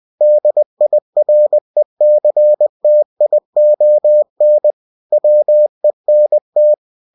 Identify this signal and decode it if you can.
Morse code, heard as a keyed tone, about 20 wpm, 600 Hz. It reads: DIRECTION WENT
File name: cw-books / winnie-the-pooh / winnie-20wpm-page-0049